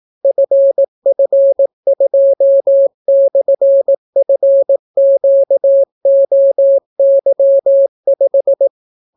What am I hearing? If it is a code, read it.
Morse code, heard as a keyed tone, about 18 wpm, 550 Hz. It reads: FF2/FQOY5